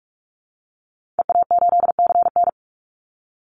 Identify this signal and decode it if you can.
Morse code, heard as a keyed tone, about 44 wpm, 700 Hz. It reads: EA8CN
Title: Call Signs Repeated 3x - 45wpm